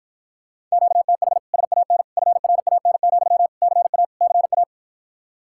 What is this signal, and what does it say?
Morse code, heard as a keyed tone, about 54 wpm, 700 Hz. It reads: QTH SAN FRAN, CA CA